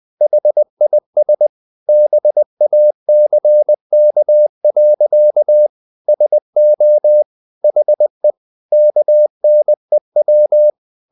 Morse code, heard as a keyed tone, about 20 wpm, 600 Hz. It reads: HIS BACK. SO HE KNEW
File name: the old man and the sea / the old man and the sea part 1 20wpm